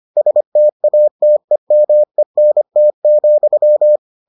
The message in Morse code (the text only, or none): STATEMENT,